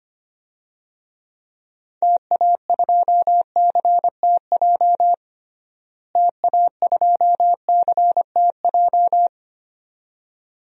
Morse code, heard as a keyed tone, about 25 wpm, 700 Hz. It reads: TA2CTJ TA2CTJ
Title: DXCC-100 Call Signs - 6 Characters - Repeated 2x - 25wpm